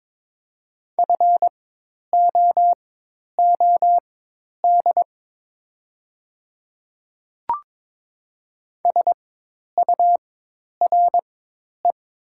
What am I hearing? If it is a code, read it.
Morse code, heard as a keyed tone, about 22 wpm, spaced as for 10 wpm, 700 Hz. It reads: FOOD SURE
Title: 4x Character Spacing - Top 300 Words - 22wpm